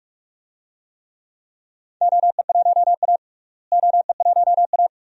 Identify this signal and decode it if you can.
Morse code, heard as a keyed tone, about 41 wpm, 700 Hz. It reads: OE1A OE1A
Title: DXCC-100 Call Signs - 4 Characters - Repeated 2x - 45wpm